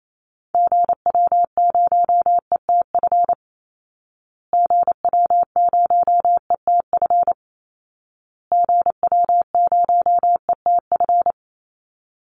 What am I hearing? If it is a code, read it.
Morse code, heard as a keyed tone, about 28 wpm, 700 Hz. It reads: GW0ETF GW0ETF GW0ETF
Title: Call Signs Repeated 2x - 28wpm